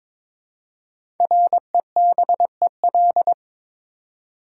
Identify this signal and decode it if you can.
Morse code, heard as a keyed tone, about 22 wpm, 700 Hz. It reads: REBEL
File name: 5 Letter Words - First 3 Letters Given - 22wpm